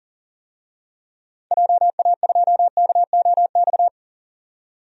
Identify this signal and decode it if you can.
Morse code, heard as a keyed tone, about 40 wpm, 700 Hz. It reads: JA2KOX
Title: DXCC-100 Call Signs - 6 Characters - 40wpm